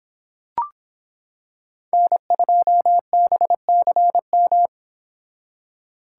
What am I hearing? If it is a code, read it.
Morse code, heard as a keyed tone, about 26 wpm, 700 Hz. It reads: N2BCM